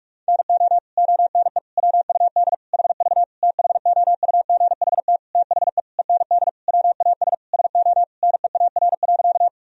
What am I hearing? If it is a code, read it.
Morse code, heard as a keyed tone, about 42 wpm, 700 Hz. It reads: NO ONE WUD HV THOUGHT THE END WAS SO NEAR.
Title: Sentences from Top 300 Words and Abbreviations 45wpm